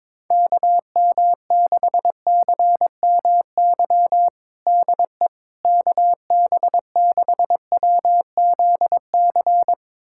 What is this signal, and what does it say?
Morse code, heard as a keyed tone, about 22 wpm, 700 Hz. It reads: KM6CMY DE KB6WZC
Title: USA Callsign DE Callsign - 6 Characters - 22wpm